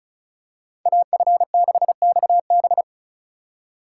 Morse code, heard as a keyed tone, about 36 wpm, 700 Hz. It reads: AF6XB
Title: USA Call Signs - 5 Characters - 3x Character Spacing - 35wpm